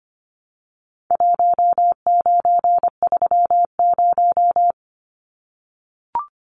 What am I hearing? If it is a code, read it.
Morse code, heard as a keyed tone, about 25 wpm, 700 Hz. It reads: 1930
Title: Numbers - Contest Style - 4-Digit 25wpm